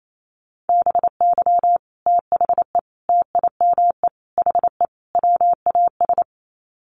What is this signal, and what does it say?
Morse code, heard as a keyed tone, about 28 wpm, 700 Hz. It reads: BY THE TIME HE WAS
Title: Sets of 5 Words Encore 28wpm